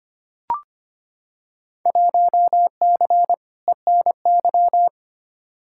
Morse code, heard as a keyed tone, about 25 wpm, 700 Hz. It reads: 1C ENY
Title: Field Day Exchanges Encore - 25wpm